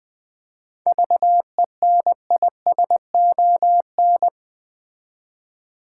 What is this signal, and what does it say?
Morse code, heard as a keyed tone, about 20 wpm, 690 Hz. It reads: VENISON
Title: Difficult Words with Hints - 20wpm